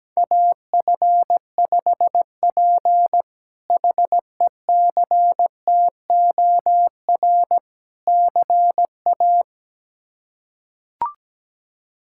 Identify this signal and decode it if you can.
Morse code, heard as a keyed tone, about 17 wpm, 700 Hz. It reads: AF5P HECTOR CA
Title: K1USN SST QSO Practice - 17wpm